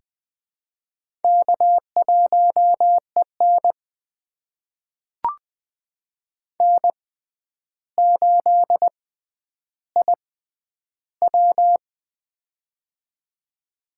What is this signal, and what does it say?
Morse code, heard as a keyed tone, about 20 wpm, spaced as for 10 wpm, 700 Hz. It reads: K1EN N8IW
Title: USA Call Signs - 4 Characters - 6x Character Spacing - 20wpm